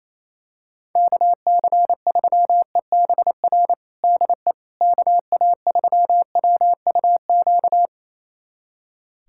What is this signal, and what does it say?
Morse code, heard as a keyed tone, about 28 wpm, 700 Hz. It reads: KC3EBR DE KA3WUQ
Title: USA Callsign DE Callsign - 6 Characters - 28wpm